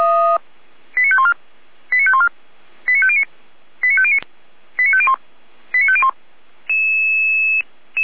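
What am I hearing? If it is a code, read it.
Five-tone selective call (ZVEI), heard as quick runs of five tones, tones of about 70 ms. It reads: 89514 89514 89509 89509 89501 89501
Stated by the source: ZVEI five-tone call 89150 received via SDR.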